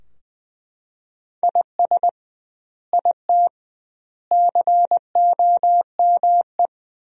Morse code, heard as a keyed tone, about 21 wpm, 700 Hz. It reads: IS IT COME
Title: 2x Word-Spacing - Sentences from Top 500 Words - 20wpm